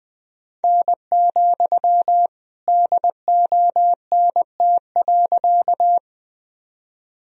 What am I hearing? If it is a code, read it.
Morse code, heard as a keyed tone, about 20 wpm, 700 Hz. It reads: N, DONT.
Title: Sentences from Top 300 Words and Abbreviations 20wpm